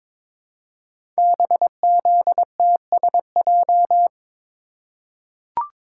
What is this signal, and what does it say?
Morse code, heard as a keyed tone, about 22 wpm, 700 Hz. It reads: BZTSJ
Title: Code Groups - 5 Letters - 22wpm